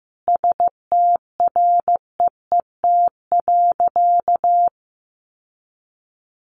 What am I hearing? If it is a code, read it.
Morse code, heard as a keyed tone, about 15 wpm, 700 Hz. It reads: STREET.
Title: Sentences from Top 1200 Words 15wpm